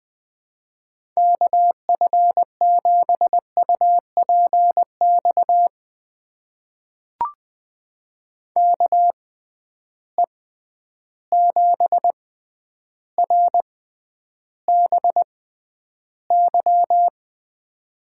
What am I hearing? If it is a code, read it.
Morse code, heard as a keyed tone, about 21 wpm, spaced as for 10 wpm, 700 Hz. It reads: KF7UPX KE7RBY
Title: USA Call Signs - 6 Characters - 6x Character Spacing - 20wpm